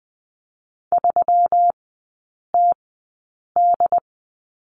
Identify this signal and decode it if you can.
Morse code, heard as a keyed tone, about 20 wpm, spaced as for 8 wpm, 690 Hz. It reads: 3TD
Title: 191 Lesson 31e - Letter v - Three Characters 20wpm